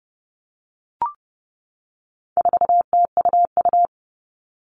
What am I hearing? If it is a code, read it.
Morse code, heard as a keyed tone, about 30 wpm, 700 Hz. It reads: 4TUU
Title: All Cut Numbers - 4-Digit 30wpm